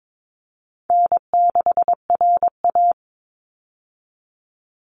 Morse code, heard as a keyed tone, about 22 wpm, 700 Hz. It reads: N6RA
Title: Easy Call Signs - 22wpm